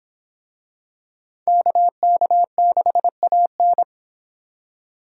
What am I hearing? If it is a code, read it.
Morse code, heard as a keyed tone, about 25 wpm, 700 Hz. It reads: KK6AN